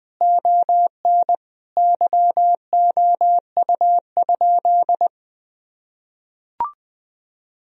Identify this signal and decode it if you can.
Morse code, heard as a keyed tone, about 20 wpm, 700 Hz. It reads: ON YOU?
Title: Sentences from Top 1000 Words - Review Part 4 - 20wpm